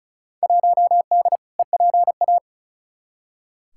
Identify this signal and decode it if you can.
Morse code, heard as a keyed tone, about 36 wpm, 700 Hz. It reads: 1D EPA
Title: Field Day Exchanges - 35wpm